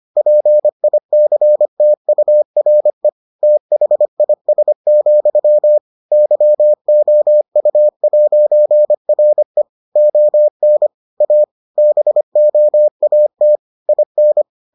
Morse code, heard as a keyed tone, about 25 wpm, 600 Hz. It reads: PICTURE THIS, YOU'RE ON A BOAT IN